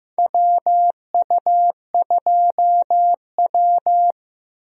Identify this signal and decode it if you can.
Morse code, heard as a keyed tone, about 15 wpm, 700 Hz. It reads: WU2W
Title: USA Call Signs - 4 Characters - Repeated 3x - 15wpm